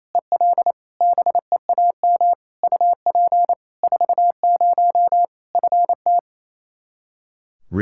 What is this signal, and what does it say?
Morse code, heard as a keyed tone, about 28 wpm, 700 Hz. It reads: EL BEAM UP 40 FT
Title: Practice QSO - Exchange 2 - Info - 28wpm